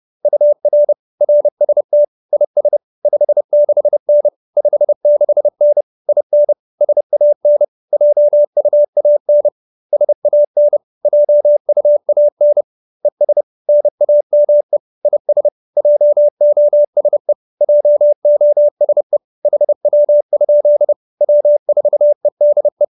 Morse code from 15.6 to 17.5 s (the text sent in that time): JOSE